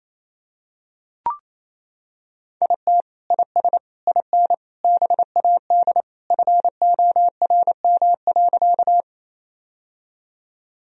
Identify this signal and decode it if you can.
Morse code, heard as a keyed tone, about 28 wpm, 700 Hz. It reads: IT IS IN BAD FORM.
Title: Sentences from Top 1000 Words - Review Part 1 - 28wpm